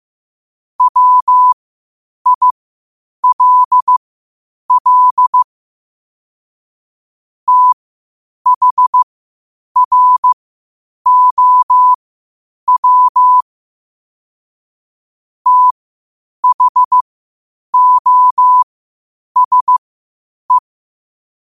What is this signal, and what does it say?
Morse code, heard as a keyed tone, about 15 wpm, spaced as for 8 wpm, 1000 Hz. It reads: WILL THROW THOSE
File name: qotdfiles / 8wpm